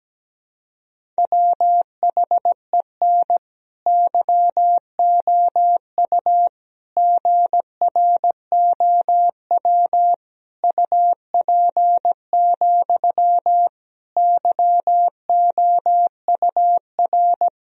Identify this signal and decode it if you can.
Morse code, heard as a keyed tone, about 17 wpm, 700 Hz. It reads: WHEN YOU GROW UP, YOUR
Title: Sentences from Top 1000 Words - Review Part 1 - 17wpm